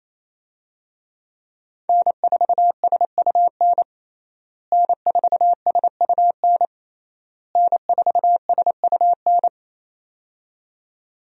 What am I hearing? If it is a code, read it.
Morse code, heard as a keyed tone, about 28 wpm, 700 Hz. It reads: N4SUN N4SUN N4SUN
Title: USA Call Signs - 5 Characters - Repeated 3x - 28wpm